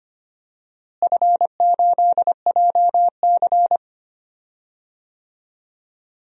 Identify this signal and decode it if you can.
Morse code, heard as a keyed tone, about 25 wpm, 700 Hz. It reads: F8JC
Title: DXCC-100 Call Signs - 4 Characters - Repeated 2x - 25wpm